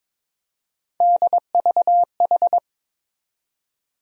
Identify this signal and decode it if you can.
Morse code, heard as a keyed tone, about 22 wpm, 700 Hz. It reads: DVH
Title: USA Call Sign Postfixes - 3 Letters 22wpm